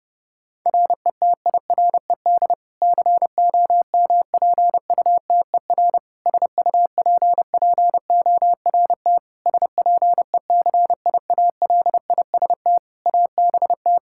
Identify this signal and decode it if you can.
Morse code, heard as a keyed tone, about 30 wpm, 700 Hz. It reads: RETIRED COMPUTER SUPPORT SPECIALIST ABT
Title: Practice QSO - Exchange 3 - More Info - 30wpm